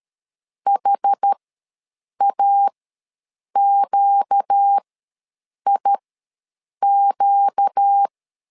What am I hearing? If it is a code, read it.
Morse code, heard as a keyed tone, about 13 wpm, spaced as for 7 wpm, 800 Hz. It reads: HAQIQ
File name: LEZIONI / 4 Q ws 12wpm 1912